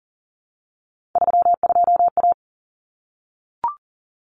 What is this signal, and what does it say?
Morse code, heard as a keyed tone, about 40 wpm, 700 Hz. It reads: 32A
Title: Common Cut Numbers - 3-Digit 40wpm